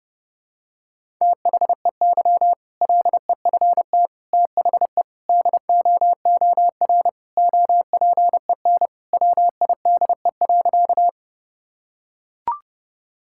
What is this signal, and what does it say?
Morse code, heard as a keyed tone, about 30 wpm, 700 Hz. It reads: THEY LEFT THE DOOR OPEN WIDE.
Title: Sentences from Top 1000 Words - Review Part 6 - 30wpm